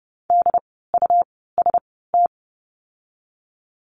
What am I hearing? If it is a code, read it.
Morse code, heard as a keyed tone, about 30 wpm, spaced as for 17 wpm, 700 Hz. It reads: DUST
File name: Lesson 17f - Letter u - Words 30wpm